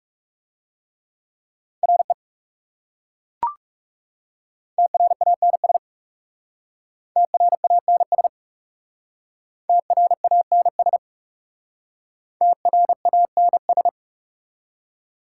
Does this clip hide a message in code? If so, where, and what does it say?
Yes: Morse code from 1.8 to 13.9 s, heard as a keyed tone, about 39 wpm, 690 Hz. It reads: RE TRANS TRANS TRANS TRANS